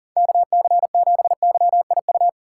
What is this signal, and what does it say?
Morse code, heard as a keyed tone, about 50 wpm, 700 Hz. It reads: KC7YIU